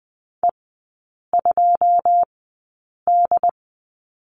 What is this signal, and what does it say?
Morse code, heard as a keyed tone, about 20 wpm, spaced as for 8 wpm, 700 Hz. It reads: E2D
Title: 133 Lesson 22e - Question Mark - Three Characters 20wpm